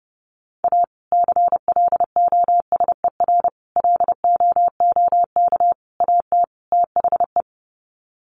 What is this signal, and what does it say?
Morse code, heard as a keyed tone, about 30 wpm, 700 Hz. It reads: A CLOSER LOOK AT THE
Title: Sets of 5 Words Encore 30wpm